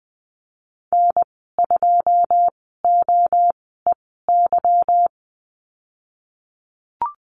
Morse code, heard as a keyed tone, about 19 wpm, spaced as for 14 wpm, 700 Hz. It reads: N2OEY
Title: 147 Lesson 24g - Letter y - Call Signs 20wpm